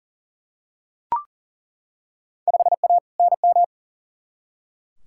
Morse code, heard as a keyed tone, about 40 wpm, 700 Hz. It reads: 5A NM
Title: Field Day Exchanges - 40wpm